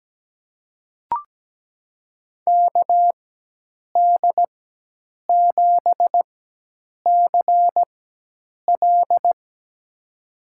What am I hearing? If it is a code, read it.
Morse code, heard as a keyed tone, about 17 wpm, spaced as for 8 wpm, 700 Hz. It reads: KD7CL
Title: USA Call Signs - 5 Characters - 4x Character Spacing - 17wpm